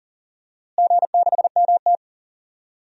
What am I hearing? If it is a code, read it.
Morse code, heard as a keyed tone, about 40 wpm, 700 Hz. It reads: G6MT
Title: DXCC-100 Call Signs - 4 Characters - Repeated 3x - 40wpm